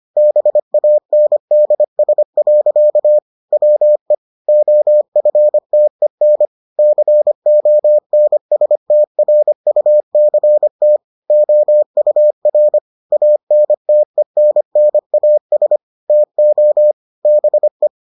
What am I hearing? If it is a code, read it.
Morse code, heard as a keyed tone, about 25 wpm, 600 Hz. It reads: BANDS. WE OFTEN CONSTRUCT OUR ANTENNAS TO BE